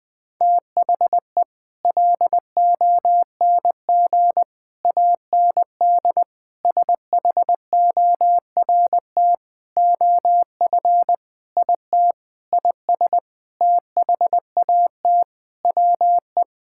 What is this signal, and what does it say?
Morse code, heard as a keyed tone, about 20 wpm, 700 Hz. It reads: THE LONG AND SHORT OF IT IS THAT WE
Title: Sentences with 8-Word Phrases - 20wpm